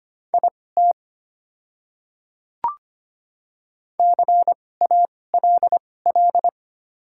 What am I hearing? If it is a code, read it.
Morse code, heard as a keyed tone, about 25 wpm, spaced as for 18 wpm, 700 Hz. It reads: IT CALL
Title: 2x Character Spacing - Top 200 Words - 25wpm